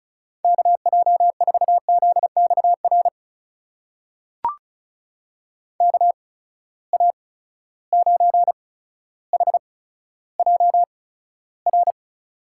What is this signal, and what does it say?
Morse code, heard as a keyed tone, about 36 wpm, spaced as for 15 wpm, 700 Hz. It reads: KJ4ZXR KA9HJR
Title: USA Call Signs - 6 Characters - 8x Character Spacing - 35wpm